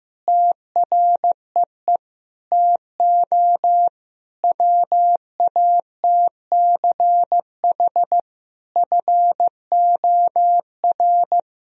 Morse code, heard as a keyed tone, about 16 wpm, 700 Hz. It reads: TREE TO WATCH FOR
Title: Sentences from Top 1000 Words - Review Part 1 - 15wpm